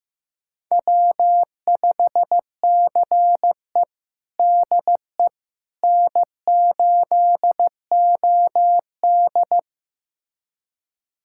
Morse code, heard as a keyed tone, about 15 wpm, 700 Hz. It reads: W5CE DE N8OD
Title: USA Callsign DE Callsign - 4 Characters - 15wpm